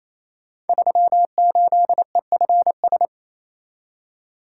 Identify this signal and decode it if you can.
Morse code, heard as a keyed tone, about 28 wpm, 700 Hz. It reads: 38EFS
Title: Code Groups - 5 Letter:Numbers - 28wpm